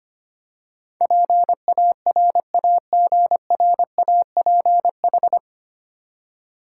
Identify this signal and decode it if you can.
Morse code, heard as a keyed tone, about 25 wpm, 700 Hz. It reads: PARAGRAPH
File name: Nine Letter Words - 25wpm